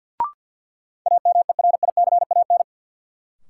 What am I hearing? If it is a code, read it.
Morse code, heard as a keyed tone, about 47 wpm, 700 Hz. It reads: AMERICAN
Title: Sets of 2 Words - New Word Drill - 50wpm